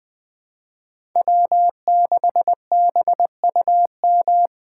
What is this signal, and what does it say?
Morse code, heard as a keyed tone, about 19 wpm, 700 Hz. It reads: W6BUM